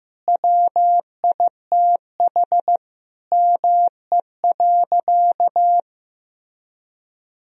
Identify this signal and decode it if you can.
Morse code, heard as a keyed tone, about 16 wpm, 700 Hz. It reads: WITH ME.
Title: Sentences from Top 1000 Words - Review Part 3 - 15wpm